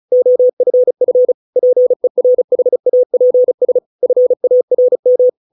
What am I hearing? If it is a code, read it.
Morse code, heard as a keyed tone, about 34 wpm, 500 Hz. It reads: OFF PERHAPS FARM